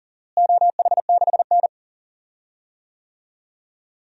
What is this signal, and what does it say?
Morse code, heard as a keyed tone, about 40 wpm, 700 Hz. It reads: OH6N
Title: DXCC-100 Call Signs - 4 Characters - Repeated 3x - 40wpm